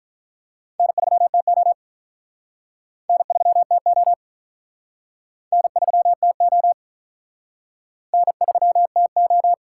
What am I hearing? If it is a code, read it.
Morse code, heard as a keyed tone, about 43 wpm, 700 Hz. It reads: N3TO N3TO N3TO N3TO
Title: Speed Racing - USA Call Signs 35wpm